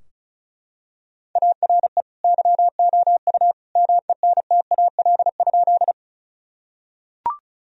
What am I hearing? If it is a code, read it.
Morse code, heard as a keyed tone, about 36 wpm, 700 Hz. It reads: ARE YOU MENTAL?
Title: Sentences from Top 1000 Words - Review Part 4 - 35wpm